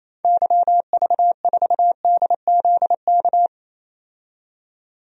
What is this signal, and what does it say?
Morse code, heard as a keyed tone, about 28 wpm, 700 Hz. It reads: YV4DZK
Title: DXCC-100 Call Signs - 6 Characters - 28wpm